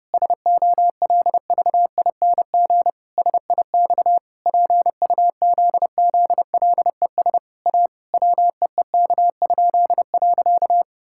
Morse code, heard as a keyed tone, about 30 wpm, 700 Hz. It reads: SOLVING SIX PUZZLES A WEEK?.
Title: Pangram Challenge v2 - 30wpm